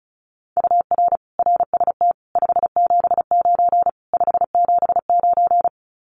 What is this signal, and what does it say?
Morse code, heard as a keyed tone, about 36 wpm, 700 Hz. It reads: UR RST 579 579
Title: Common QSO Phrases Encore 35wpm